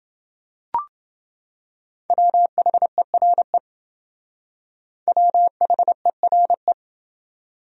Morse code, heard as a keyed tone, about 28 wpm, 700 Hz. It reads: WHERE WHERE